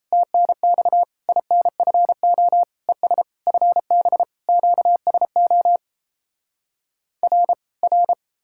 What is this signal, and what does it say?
Morse code, heard as a keyed tone, about 37 wpm, 700 Hz. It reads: TNX INFO ES FB QSO R R